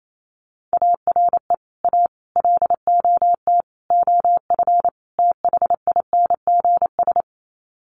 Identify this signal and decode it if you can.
Morse code, heard as a keyed tone, about 28 wpm, 700 Hz. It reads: ARE A LOT OF THINGS
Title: Sets of 5 Words Encore2 28wpm